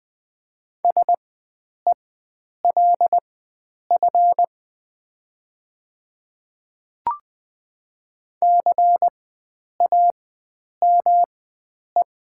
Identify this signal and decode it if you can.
Morse code, heard as a keyed tone, about 20 wpm, spaced as for 9 wpm, 700 Hz. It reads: SELF CAME
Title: 4x Character Spacing - Top 500 Words - 20wpm